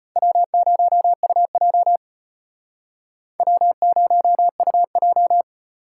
Morse code, heard as a keyed tone, about 33 wpm, 700 Hz. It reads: W0UJ W0UJ